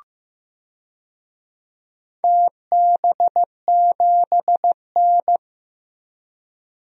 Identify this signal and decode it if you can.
Morse code, heard as a keyed tone, about 15 wpm, 700 Hz. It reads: TB7N
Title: DXCC-100 Call Signs - 4 Characters - 15wpm